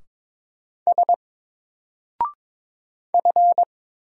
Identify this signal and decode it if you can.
Morse code, heard as a keyed tone, about 22 wpm, 700 Hz. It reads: S F
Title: Single Character (Letter:Number:Punctuation:Prosign) 22wpm